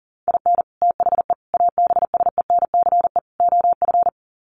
Morse code, heard as a keyed tone, about 40 wpm, 700 Hz. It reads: IN THE ABSENCE OF
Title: Sets of 4 Words 40wpm